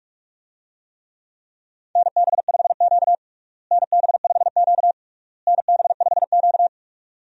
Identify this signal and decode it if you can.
Morse code, heard as a keyed tone, about 46 wpm, 700 Hz. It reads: NB5Q NB5Q NB5Q
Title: USA Call Signs - 4 Characters - Repeated 3x - 45wpm